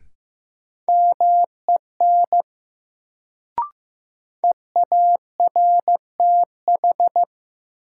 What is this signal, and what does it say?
Morse code, heard as a keyed tone, about 15 wpm, 700 Hz. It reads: MEN EARTH
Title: Sentences from Top 200 Words - New Word Drill - 15wpm